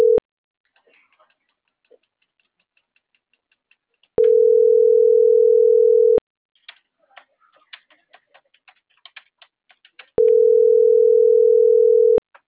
Ringback tone (North America), 440 + 480 Hz.